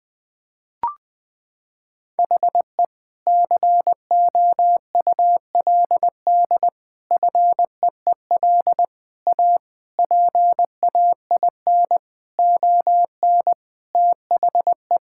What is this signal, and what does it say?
Morse code, heard as a keyed tone, about 20 wpm, 700 Hz. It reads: HE COULD FEEL A PAIN ON THE
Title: Sentences from Top 1600 Words 20wpm